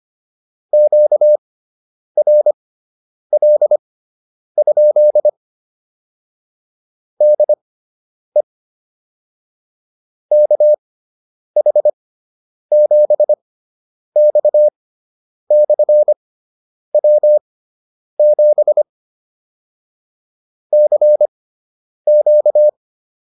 Morse code, heard as a keyed tone, about 25 wpm, spaced as for 9 wpm, 600 Hz. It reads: QRL? DE KH7X/W7 CQ